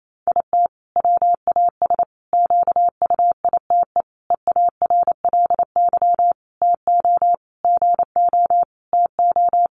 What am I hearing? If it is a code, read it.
Morse code, heard as a keyed tone, about 28 wpm, 700 Hz. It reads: IT WAS QUITE EARLY TO GO TO
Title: Sentences from Top 700 Words 28wpm